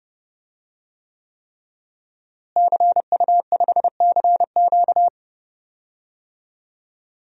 Morse code, heard as a keyed tone, about 30 wpm, 700 Hz. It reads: CU5CQ
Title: DXCC-100 Call Signs - 5 Characters - Repeated 3x - 30wpm